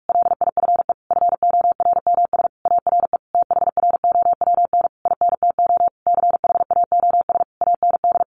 Morse code, heard as a keyed tone, about 44 wpm, 700 Hz. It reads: LIFE FORMS ARE THROWN INTO CHAOS AND
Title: A Sound Of Thunder 45wpm